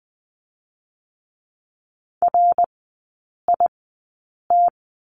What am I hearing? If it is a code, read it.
Morse code, heard as a keyed tone, about 20 wpm, spaced as for 8 wpm, 700 Hz. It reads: RIT
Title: 081 Lesson 14e - Number 2 - Three Characters 20wpm